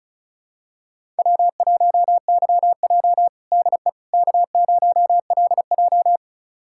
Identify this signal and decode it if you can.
Morse code, heard as a keyed tone, about 34 wpm, 700 Hz. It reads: W1YJ DE K0LJ
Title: USA Callsign DE Callsign - 4 Characters - 35wpm